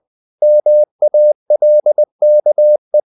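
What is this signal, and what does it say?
Morse code, heard as a keyed tone, about 20 wpm, 600 Hz. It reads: MALKE
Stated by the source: Morse code practice of short Danish words at 20 wpm, from CW med Gnister.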